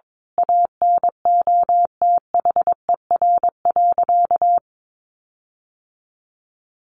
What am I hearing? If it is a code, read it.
Morse code, heard as a keyed tone, about 22 wpm, 700 Hz. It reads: ANOTHER.
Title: Sentences from Top 1400 Words 22wpm